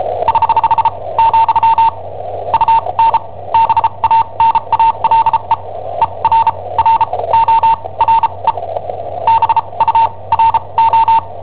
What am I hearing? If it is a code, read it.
Morse code, heard as a keyed tone, about 37 wpm, spaced as for 26 wpm, 910 Hz. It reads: <HH> , UN BANALE ERRORE BURO
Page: Cassetta esercizi preparata da I7OHP per IW5ELP